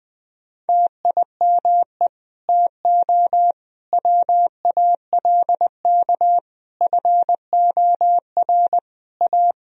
Morse code, heard as a keyed tone, about 20 wpm, 700 Hz. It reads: TIME TO WALK FOR A
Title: Sentences from Top 1000 Words - Review Part 1 - 20wpm